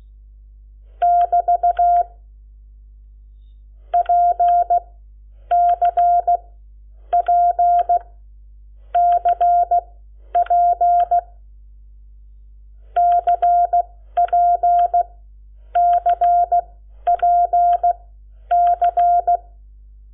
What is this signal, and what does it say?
Morse code, heard as a keyed tone, about 16 wpm, spaced as for 9 wpm, 700 Hz. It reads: <BT> PCPCP CPCPC
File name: Lecciones / CLASE Nº7